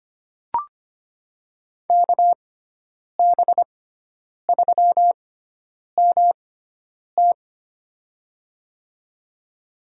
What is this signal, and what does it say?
Morse code, heard as a keyed tone, about 25 wpm, spaced as for 9 wpm, 700 Hz. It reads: KB3MT
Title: USA Call Signs - 5 Characters - 6x Character Spacing - 25wpm